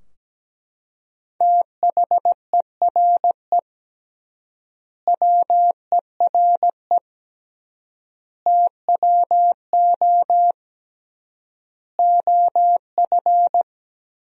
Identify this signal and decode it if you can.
Morse code, heard as a keyed tone, about 17 wpm, 700 Hz. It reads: THERE WERE TWO OF
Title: 3x Word-Spacing - Sentences from Top 100 Words - 17wpm